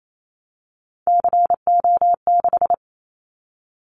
Morse code, heard as a keyed tone, about 28 wpm, 700 Hz. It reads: CO6
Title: Callsign Prefixes for Top 100 DXCC - 28wpm